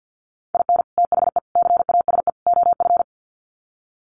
Morse code, heard as a keyed tone, about 50 wpm, 700 Hz. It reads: IN THE CASE OF
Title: Sets of 4 Words 50wpm